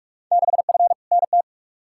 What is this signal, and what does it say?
Morse code, heard as a keyed tone, about 47 wpm, 700 Hz. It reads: 6F NT
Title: Field Day Exchanges Encore - 45wpm